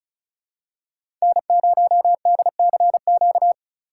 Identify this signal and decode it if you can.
Morse code, heard as a keyed tone, about 36 wpm, 700 Hz. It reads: N0DCQ
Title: USA Call Signs - 5 Characters - 3x Character Spacing - 35wpm